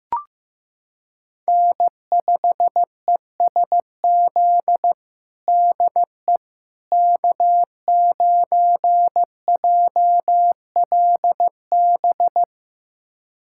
Morse code, heard as a keyed tone, about 16 wpm, 700 Hz. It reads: N5ESZ DE K9JLB
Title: USA Callsign DE Callsign - 5 Characters - 15wpm